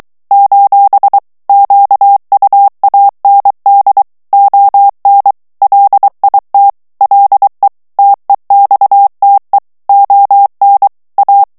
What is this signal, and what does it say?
Morse code, heard as a keyed tone, about 23 wpm, 800 Hz. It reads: : QUAND ON LIT LE TEXTE ON A
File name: morse mp3 / FRANCAIS TEXTE CLAIR 1 VITESSE 22 MOTS MINUTES